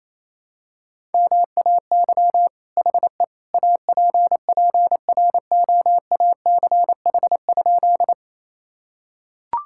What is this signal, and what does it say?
Morse code, heard as a keyed tone, about 28 wpm, 700 Hz. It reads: MAY HE APPROACH?
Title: Sentences from Top 1000 Words - Review Part 6 - 28wpm